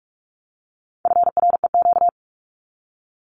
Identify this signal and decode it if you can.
Morse code, heard as a keyed tone, about 44 wpm, 700 Hz. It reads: FREQ
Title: Abbreviations 45wpm